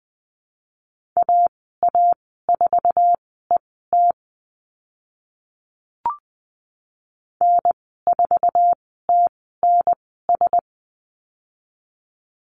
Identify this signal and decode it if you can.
Morse code, heard as a keyed tone, about 20 wpm, spaced as for 15 wpm, 700 Hz. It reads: AA4ET N4TNS
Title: 055 Lesson 10g - Letter r - Call Signs 20wpm